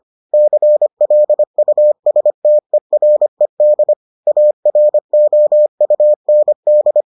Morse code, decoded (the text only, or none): CLUSTERED AROUND